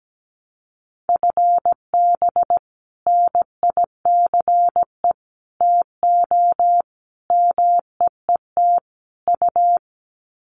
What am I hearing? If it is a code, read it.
Morse code, heard as a keyed tone, about 17 wpm, 700 Hz. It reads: FB NICE TO MEET U
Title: Common QSO Phrases 17wpm